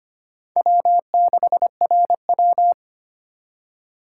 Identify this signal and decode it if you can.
Morse code, heard as a keyed tone, about 25 wpm, 700 Hz. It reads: W6RW
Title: USA Call Signs - 4 Characters - 4x Character Spacing - 25wpm